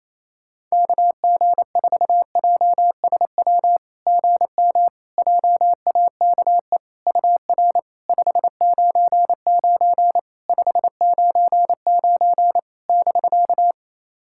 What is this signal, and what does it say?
Morse code, heard as a keyed tone, about 28 wpm, 700 Hz. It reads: KG4JSW GM JAKE UR 599 599 <BK>
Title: SOTA QSO Practice 28wpm